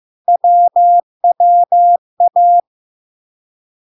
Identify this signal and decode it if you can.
Morse code, heard as a keyed tone, about 15 wpm, 700 Hz. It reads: WWA
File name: LARK-FD-2022-Log-in-CW-15wpm-700-Hz